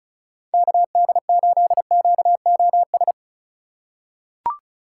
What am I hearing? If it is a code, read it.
Morse code, heard as a keyed tone, about 36 wpm, 700 Hz. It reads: KD8QOS
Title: USA Call Signs - 6 Characters - 2x Character Spacing - 35wpm